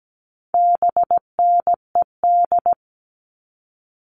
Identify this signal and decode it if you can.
Morse code, heard as a keyed tone, about 17 wpm, 700 Hz. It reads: BNED
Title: All Cut Numbers - 4-Digit 17wpm